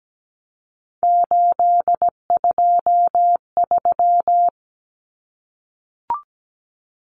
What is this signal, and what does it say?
Morse code, heard as a keyed tone, about 17 wpm, 700 Hz. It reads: 823
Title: Common Cut Numbers - 3-Digit 17wpm